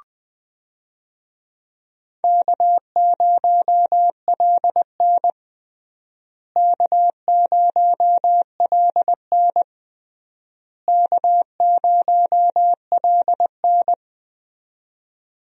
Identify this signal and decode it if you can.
Morse code, heard as a keyed tone, about 20 wpm, 700 Hz. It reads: K0LN K0LN K0LN
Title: USA Call Signs - 4 Characters - Repeated 3x - 20wpm